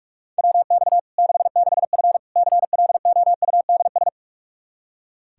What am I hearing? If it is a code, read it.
Morse code, heard as a keyed tone, about 46 wpm, 700 Hz. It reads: WX 66F CLOUDS
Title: Practice QSO - Exchange 2 - Info - 45wpm